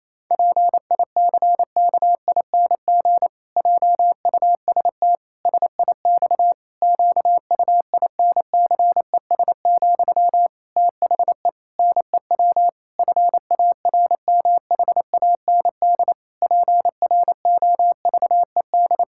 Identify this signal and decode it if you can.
Morse code, heard as a keyed tone, about 28 wpm, 700 Hz. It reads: PICKING JUST SIX QUINCES, THE NEW FARMHAND PROVED
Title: Pangram Challenge v2 - 28wpm